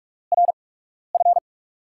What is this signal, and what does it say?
Morse code, heard as a keyed tone, about 44 wpm, spaced as for 12 wpm, 700 Hz. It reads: RF
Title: USA Call Signs - 4 Characters - 8x Character Spacing - 45wpm